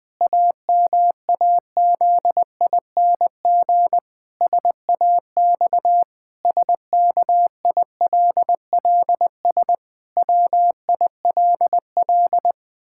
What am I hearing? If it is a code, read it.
Morse code, heard as a keyed tone, about 21 wpm, 700 Hz. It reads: AMAZING SAX SKILLS WILL
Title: Pangram Challenge v2 - 20wpm